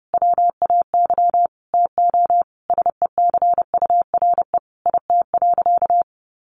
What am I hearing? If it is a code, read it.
Morse code, heard as a keyed tone, about 30 wpm, 700 Hz. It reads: WAY TO SECURE IT.
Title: Sentences from Top 1200 Words 30wpm